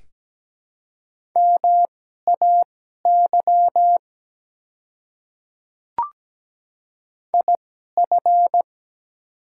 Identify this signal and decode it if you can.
Morse code, heard as a keyed tone, about 17 wpm, spaced as for 12 wpm, 700 Hz. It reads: MAY IF
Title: 2x Character Spacing - Top 200 Words - 17wpm